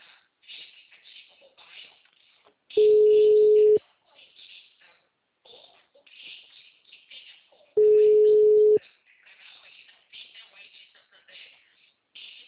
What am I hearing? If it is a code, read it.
Ringback tone (Europe, 425 Hz), with speech.